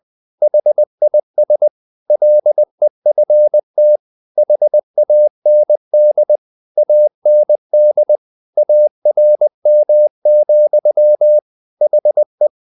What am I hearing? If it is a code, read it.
Morse code, heard as a keyed tone, about 20 wpm, 600 Hz. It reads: HIS LEFT HAND AND ARM, HE
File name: the old man and the sea / the old man and the sea part 1 20wpm